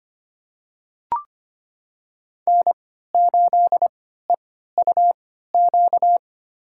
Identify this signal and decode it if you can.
Morse code, heard as a keyed tone, about 25 wpm, spaced as for 14 wpm, 700 Hz. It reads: N8EUQ
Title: USA Call Signs - 5 Characters - 3x Character Spacing - 25wpm